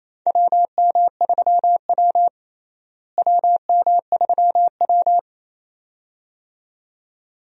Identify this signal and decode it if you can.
Morse code, heard as a keyed tone, about 28 wpm, 700 Hz. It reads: WM3W WM3W
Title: USA Call Signs - 4 Characters - Repeated 2x - 28wpm